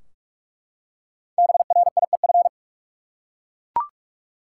Morse code, heard as a keyed tone, about 42 wpm, 700 Hz. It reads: BRIEF
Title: Five Letter Words - 45wpm